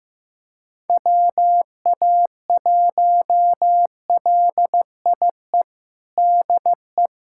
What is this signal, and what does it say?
Morse code, heard as a keyed tone, about 15 wpm, 700 Hz. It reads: WA1LIE DE
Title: USA Callsign DE Callsign - 6 Characters - 15wpm